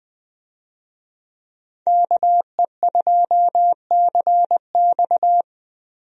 Morse code, heard as a keyed tone, about 20 wpm, 700 Hz. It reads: KE2CX
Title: USA Call Signs - 5 Characters - Repeated 2x - 20wpm